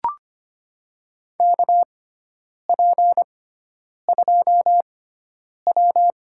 Morse code, heard as a keyed tone, about 25 wpm, spaced as for 9 wpm, 700 Hz. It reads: KP2W
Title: USA Call Signs - 4 Characters - 6x Character Spacing - 25wpm